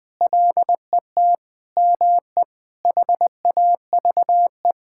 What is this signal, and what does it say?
Morse code, heard as a keyed tone, about 20 wpm, 700 Hz. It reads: LET ME HAVE
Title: Sentences from Top 1600 Words 20wpm